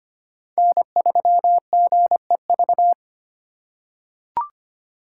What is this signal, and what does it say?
Morse code, heard as a keyed tone, about 25 wpm, 700 Hz. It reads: N3GEV
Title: Code Groups - 5 Letter:Numbers - 25wpm